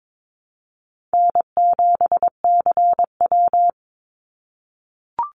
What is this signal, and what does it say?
Morse code, heard as a keyed tone, about 22 wpm, 700 Hz. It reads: N7CW
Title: NA SOTA Activator Callsigns 22wpm